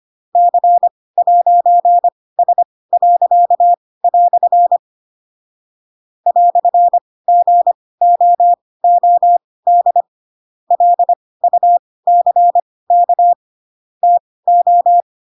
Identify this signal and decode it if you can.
Morse code, heard as a keyed tone, about 25 wpm, spaced as for 18 wpm, 700 Hz. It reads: C'S." "GOOD LUCK TO
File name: Pinocchio / Pinocchio ch2 12 25 ik1hge